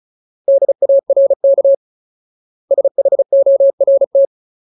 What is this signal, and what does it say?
Morse code, heard as a keyed tone, about 34 wpm, 550 Hz. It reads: DARK SHORT
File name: Ultimate Word List / 35WPM Ultimate